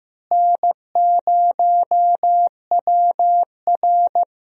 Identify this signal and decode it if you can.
Morse code, heard as a keyed tone, about 15 wpm, 700 Hz. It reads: N0WR
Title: USA Call Signs - 4 Characters - Repeated 2x - 15wpm